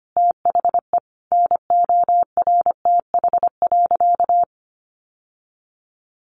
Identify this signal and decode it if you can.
Morse code, heard as a keyed tone, about 25 wpm, 700 Hz. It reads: THE NORTH.
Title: Sentences from Top 1200 Words 25wpm